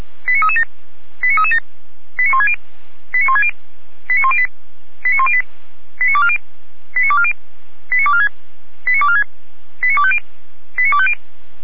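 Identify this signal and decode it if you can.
Five-tone selective call (ZVEI), heard as quick runs of five tones, tones of about 80 ms. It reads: 89337 89337 89160 89160 89108 89108 89240 89240 89246 89246 89250 89250
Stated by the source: ZVEI five-tone selective call received by SDR.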